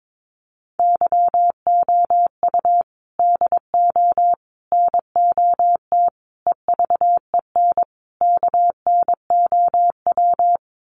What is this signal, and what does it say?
Morse code, heard as a keyed tone, about 22 wpm, 700 Hz. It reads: YOU DO NOT EVEN KNOW
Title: Sets of 5 Words Encore2 22wpm